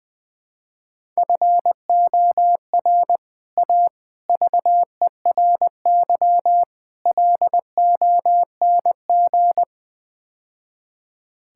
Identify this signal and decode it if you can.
Morse code, heard as a keyed tone, about 21 wpm, 700 Hz. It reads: FOR A VERY LONG
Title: Build A Phrase - 5-Word Phrases - 20wpm